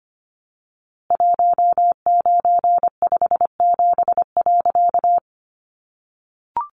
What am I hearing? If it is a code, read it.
Morse code, heard as a keyed tone, about 25 wpm, 700 Hz. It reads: 1957.
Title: A Sound Of Thunder 25wpm